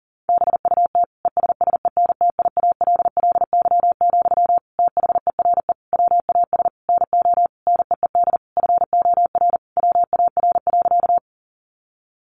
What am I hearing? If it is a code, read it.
Morse code, heard as a keyed tone, about 40 wpm, 700 Hz. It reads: BUT ESSENTIALLY, THERE WAS NO NEED FOR WAR.
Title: Sentences from Top 1300 Words 40wpm